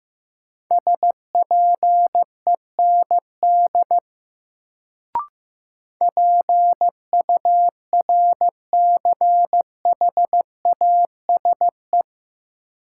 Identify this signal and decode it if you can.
Morse code, heard as a keyed tone, about 15 wpm, 700 Hz. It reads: SPEND PURCHASE
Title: Sentences from Top 800 Words - New Word Drill - 15wpm